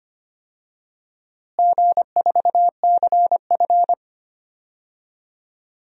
Morse code, heard as a keyed tone, about 25 wpm, 700 Hz. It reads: G4CF
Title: DXCC-100 Call Signs - 4 Characters - Repeated 3x - 25wpm